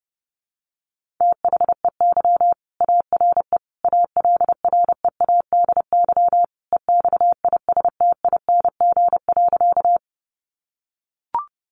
Morse code, heard as a keyed tone, about 30 wpm, 700 Hz. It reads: THEY ARE ALREADY EXISTING.
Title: Sentences from Top 1500 Words 30wpm